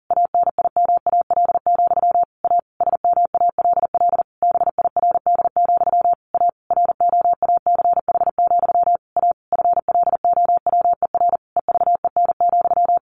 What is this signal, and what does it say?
Morse code, heard as a keyed tone, about 40 wpm, 700 Hz. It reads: ANIMAL, A SMALL BIRD, A ROACH, A FLOWER EVEN,
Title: A Sound Of Thunder 40wpm